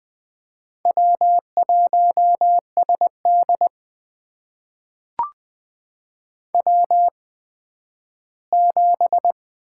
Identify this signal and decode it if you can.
Morse code, heard as a keyed tone, about 20 wpm, spaced as for 12 wpm, 700 Hz. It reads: W1SD W7
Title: USA Call Signs - 4 Characters - 8x Character Spacing - 20wpm